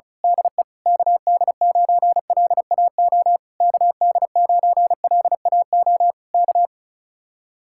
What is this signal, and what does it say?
Morse code, heard as a keyed tone, about 33 wpm, 700 Hz. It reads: DE KD9LAO KD9LAO K
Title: Speed-Racing Common QSO Phrases Encore - 35wpm